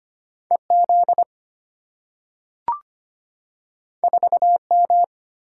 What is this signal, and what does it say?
Morse code, heard as a keyed tone, about 25 wpm, 690 Hz. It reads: EZ 4M
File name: Two Letter-Number Combinations - 25wpm